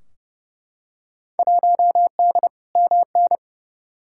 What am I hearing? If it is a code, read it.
Morse code, heard as a keyed tone, about 30 wpm, 700 Hz. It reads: 1D MN
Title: Field Day Exchanges - 30wpm